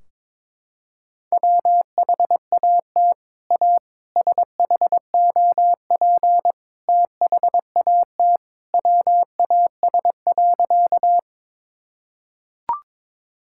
Morse code, heard as a keyed tone, about 22 wpm, 700 Hz. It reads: WHAT A SHOP THAT WAS.
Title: Sentences from Top 1600 Words 22wpm